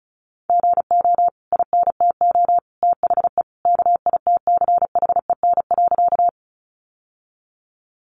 Morse code, heard as a keyed tone, about 34 wpm, 700 Hz. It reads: GO INTO THE KITCHEN.
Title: Sentences from Top 1400 Words 35wpm